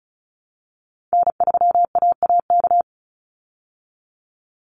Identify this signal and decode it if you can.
Morse code, heard as a keyed tone, about 34 wpm, 700 Hz. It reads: N3AAK
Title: Call Signs Repeated 3x - 35wpm